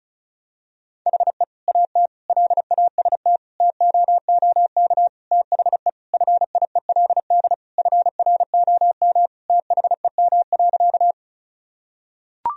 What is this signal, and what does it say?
Morse code, heard as a keyed tone, about 34 wpm, 700 Hz. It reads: HE AT LAST TOOK THE FIELD FROM THEM.
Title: Sentences from Top 1000 Words - Review Part 2 - 35wpm